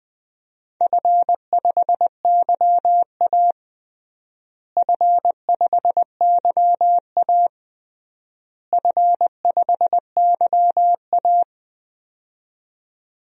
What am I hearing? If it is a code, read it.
Morse code, heard as a keyed tone, about 21 wpm, 700 Hz. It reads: F5YA F5YA F5YA
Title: DXCC-100 Call Signs - 4 Characters - Repeated 3x - 20wpm